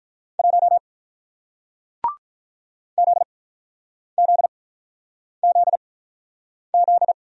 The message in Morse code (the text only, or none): 1 ZZZZ